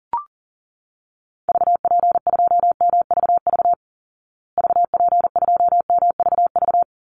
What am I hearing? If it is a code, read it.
Morse code, heard as a keyed tone, about 40 wpm, 700 Hz. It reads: VP2MVV VP2MVV
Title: Call Signs Repeated 2x - 40wpm